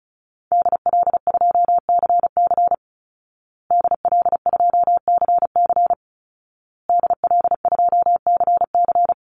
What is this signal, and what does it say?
Morse code, heard as a keyed tone, about 34 wpm, 700 Hz. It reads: DL2CC DL2CC DL2CC
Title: Call Signs Repeated 2x - 35wpm